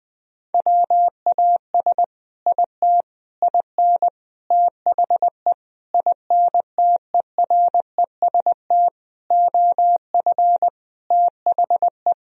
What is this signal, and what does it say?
Morse code, heard as a keyed tone, about 19 wpm, 700 Hz. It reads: WAS IT IN THE INTEREST OF THE
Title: Sentences from Top 1000 Words - Review Part 4 - 20wpm